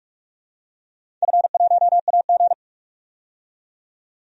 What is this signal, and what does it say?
Morse code, heard as a keyed tone, about 46 wpm, 700 Hz. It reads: F1AG
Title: DXCC-100 Call Signs - 4 Characters - 45wpm